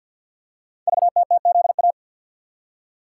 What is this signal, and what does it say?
Morse code, heard as a keyed tone, about 48 wpm, 700 Hz. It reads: VTTZA
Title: Code Groups - 5 Letters - 50wpm